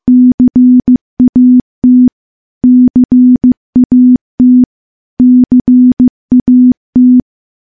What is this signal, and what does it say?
Morse code, heard as a keyed tone, about 15 wpm, 250 Hz. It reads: CAT CAT CAT
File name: morse-training-15WPM